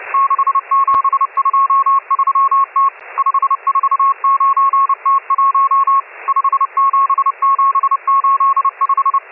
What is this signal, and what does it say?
Morse code, heard as a keyed tone, about 30 wpm, 1090 Hz, with radio noise. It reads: 6723T 549T1 57785
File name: M12- -1010utc